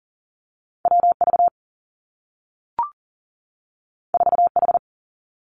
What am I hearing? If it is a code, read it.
Morse code, heard as a keyed tone, about 42 wpm, 700 Hz. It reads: W V 4 H
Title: Two Challenging Letter-Number Combinations 40wpm